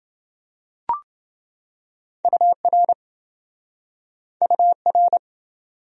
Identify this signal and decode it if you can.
Morse code, heard as a keyed tone, about 28 wpm, 700 Hz. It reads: UR UR